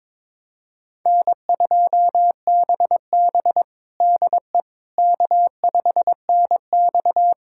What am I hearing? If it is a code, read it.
Morse code, heard as a keyed tone, about 22 wpm, 700 Hz. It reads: N2BB DE K5NX
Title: USA Callsign DE Callsign - 4 Characters - 22wpm